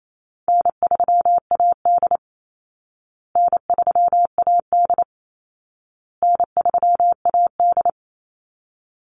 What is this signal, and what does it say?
Morse code, heard as a keyed tone, about 28 wpm, 700 Hz. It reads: N3AD N3AD N3AD
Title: Call Signs Repeated 2x - 28wpm